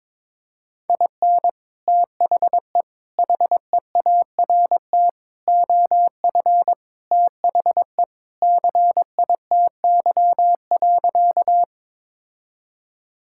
Morse code, heard as a keyed tone, about 22 wpm, 700 Hz. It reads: IN THE HEART OF THE CITY.
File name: practice / 22wpm / Sentences with 3-Word Phrases - 22wpm